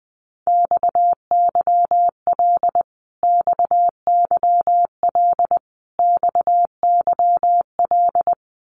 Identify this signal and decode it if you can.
Morse code, heard as a keyed tone, about 20 wpm, 700 Hz. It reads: XYL XYL XYL
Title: Abbreviations Repeated 3x - 20wpm